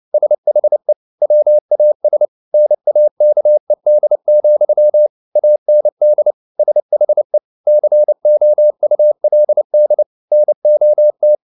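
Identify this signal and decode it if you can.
Morse code, heard as a keyed tone, about 29 wpm, 600 Hz. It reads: SHE WAS NAKED, AND SHE COULD NOT
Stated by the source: Morse code at 29 wpm, sending Ken Follett's novel Never in English, from CW med Gnister.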